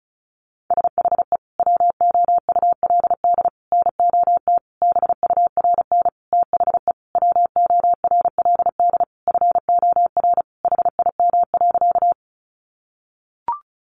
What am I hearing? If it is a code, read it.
Morse code, heard as a keyed tone, about 34 wpm, 700 Hz. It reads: SHE WOULD NOT BURN THE WORLD FOR HIM.
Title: Sentences from Top 1300 Words 35wpm